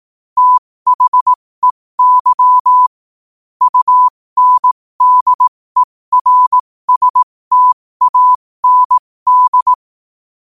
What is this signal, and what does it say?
Morse code, heard as a keyed tone, about 18 wpm, spaced as for 15 wpm, 1000 Hz. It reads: THEY UNDERSTAND
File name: qotdfiles / 15wpm